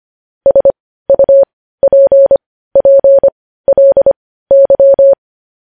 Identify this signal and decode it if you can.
Morse code, heard as a keyed tone, about 25 wpm, spaced as for 15 wpm, 550 Hz. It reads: SUPPLY
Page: News Headlines in Morse Code at 15 WPM